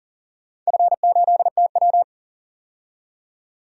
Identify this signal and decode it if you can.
Morse code, heard as a keyed tone, about 40 wpm, 700 Hz. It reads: F8TW
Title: DXCC-100 Call Signs - 4 Characters - 40wpm